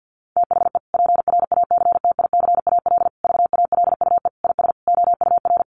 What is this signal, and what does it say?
Morse code, heard as a keyed tone, about 50 wpm, 700 Hz. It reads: THE PRACTICAL VALUE IS OUR
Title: Sentences from Top 1400 Words 50wpm